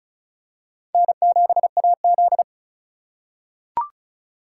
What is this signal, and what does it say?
Morse code, heard as a keyed tone, about 36 wpm, 700 Hz. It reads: N7AZ
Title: K1USN SST QSO Practice - 35wpm